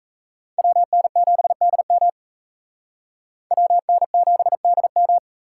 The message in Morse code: WN7DM WN7DM

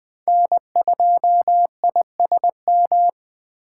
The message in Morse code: N2ISM

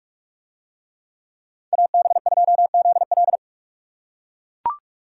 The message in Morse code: AB2ZL

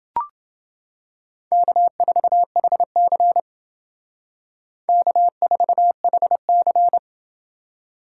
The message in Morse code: K4HC K4HC

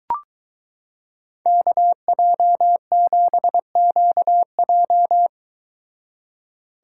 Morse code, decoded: KJ7QJ

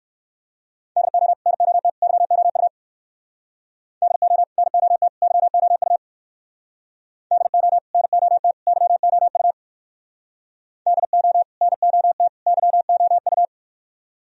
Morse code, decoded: DO NOT YOU DO NOT YOU DO NOT YOU DO NOT YOU

